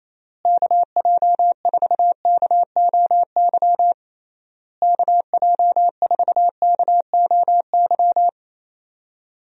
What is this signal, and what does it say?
Morse code, heard as a keyed tone, about 27 wpm, 700 Hz. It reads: KJ4KOY KJ4KOY
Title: USA Call Signs - 6 Characters - Repeated 3x - 28wpm